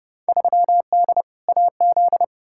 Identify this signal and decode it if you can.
Morse code, heard as a keyed tone, about 30 wpm, 700 Hz. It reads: 3D AZ
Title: Field Day Exchanges - 30wpm